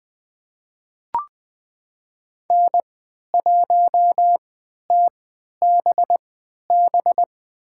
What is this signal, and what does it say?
Morse code, heard as a keyed tone, about 20 wpm, spaced as for 11 wpm, 700 Hz. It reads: N1TBB
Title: USA Call Signs - 5 Characters - 3x Character Spacing - 20wpm